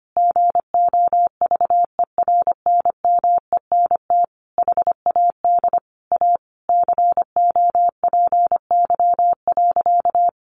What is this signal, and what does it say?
Morse code, heard as a keyed tone, about 25 wpm, 700 Hz. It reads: GOVERNMENT HAD A COPY.
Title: Sentences from Top 1400 Words 25wpm